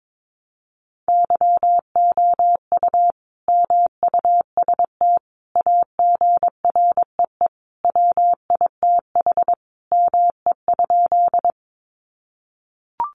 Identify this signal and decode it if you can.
Morse code, heard as a keyed tone, about 22 wpm, 700 Hz. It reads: YOU MUST AGREE WITH ME?
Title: Sentences from Top 1100 Words 22wpm